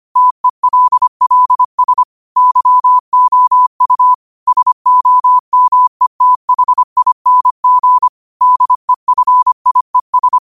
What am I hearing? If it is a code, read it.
Morse code, heard as a keyed tone, about 25 wpm, 1000 Hz. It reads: TELLS YOU SOMETHING DEFIES
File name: qotdfiles / 25wpm